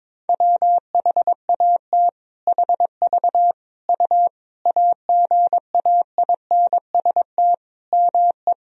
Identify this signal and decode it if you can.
Morse code, heard as a keyed tone, about 22 wpm, 700 Hz. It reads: WHAT HV U AGAINST ME
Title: Sentences from Top 400 Words and Abbreviations 22wpm